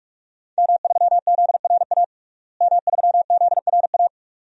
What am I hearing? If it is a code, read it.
Morse code, heard as a keyed tone, about 48 wpm, 700 Hz. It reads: M3ZRA M3ZRA
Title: DXCC-100 Call Signs - 5 Characters - Repeated 2x - 45wpm